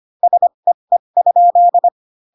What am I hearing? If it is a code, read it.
Morse code, heard as a keyed tone, about 25 wpm, spaced as for 22 wpm, 700 Hz. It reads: SEE?
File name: Pinocchio / Pinocchio ch3 18 25 ik1hge